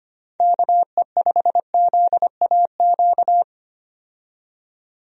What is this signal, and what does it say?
Morse code, heard as a keyed tone, about 25 wpm, 700 Hz. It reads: KE5ZAQ
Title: USA Call Signs - 6 Characters - 25wpm